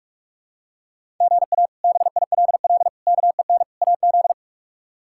Morse code, heard as a keyed tone, about 46 wpm, 700 Hz. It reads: GA BILL KEN AZ
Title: K1USN SST QSO Practice - 45wpm